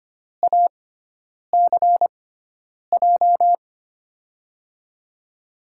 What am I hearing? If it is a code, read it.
Morse code, heard as a keyed tone, about 25 wpm, spaced as for 9 wpm, 700 Hz. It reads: ACJ
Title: USA Call Signs - 6x Character Spacing - 25wpm